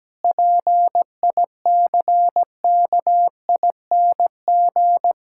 Morse code, heard as a keyed tone, about 17 wpm, 700 Hz. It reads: PICKING
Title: Pangram Challenge v2 - 17wpm